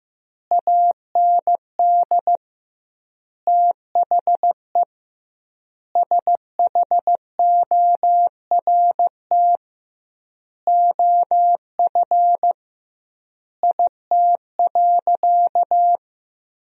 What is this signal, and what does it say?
Morse code, heard as a keyed tone, about 15 wpm, 700 Hz. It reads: AND THE SHORT OF IT.
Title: 2x Word-Spacing - Sentences from Top 400 Words - 15wpm